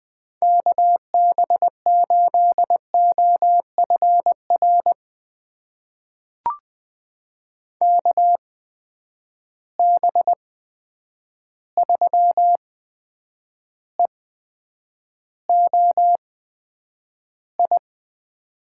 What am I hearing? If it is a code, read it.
Morse code, heard as a keyed tone, about 20 wpm, spaced as for 9 wpm, 700 Hz. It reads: KB8OFR KB3EOI
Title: USA Call Signs - 6 Characters - 8x Character Spacing - 20wpm